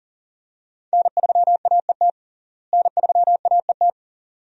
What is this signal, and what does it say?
Morse code, heard as a keyed tone, about 37 wpm, 700 Hz. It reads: N3AET N3AET
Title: USA Call Signs - 5 Characters - Repeated 2x - 40wpm